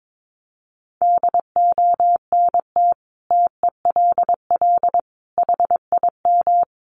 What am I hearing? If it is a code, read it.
Morse code, heard as a keyed tone, about 22 wpm, 700 Hz. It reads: DONT TELL HIM
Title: Sentences from Top 700 Words 22wpm